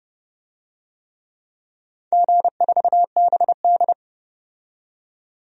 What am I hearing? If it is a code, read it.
Morse code, heard as a keyed tone, about 32 wpm, 700 Hz. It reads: G4BD
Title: DXCC-100 Call Signs - 4 Characters - 30wpm